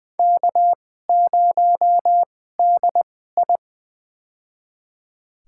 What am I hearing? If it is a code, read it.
Morse code, heard as a keyed tone, about 20 wpm, spaced as for 14 wpm, 700 Hz. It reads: K0DI
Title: USA Call Signs - 4 Characters - 2x Character Spacing - 20wpm